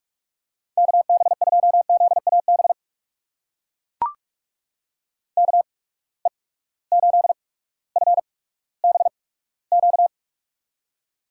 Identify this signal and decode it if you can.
Morse code, heard as a keyed tone, about 44 wpm, spaced as for 19 wpm, 700 Hz. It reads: KB2ZAB KE8FBQ
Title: USA Call Signs - 6 Characters - 8x Character Spacing - 45wpm